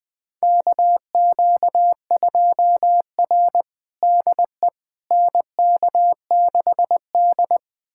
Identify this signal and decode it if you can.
Morse code, heard as a keyed tone, about 20 wpm, 700 Hz. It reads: KQ2R DE NK6D
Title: USA Callsign DE Callsign - 4 Characters - 20wpm